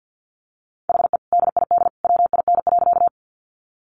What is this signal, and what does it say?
Morse code, heard as a keyed tone, about 50 wpm, 700 Hz. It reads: HE DID WIN.
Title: Sentences from Top 1500 Words 50wpm